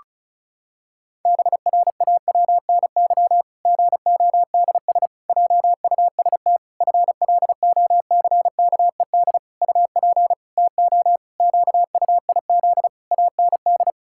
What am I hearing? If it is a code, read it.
Morse code, heard as a keyed tone, about 36 wpm, 700 Hz. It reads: BRAWNY GODS JUST FLOCKED UP TO QUIZ AND
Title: Pangram Challenge v2 - 35wpm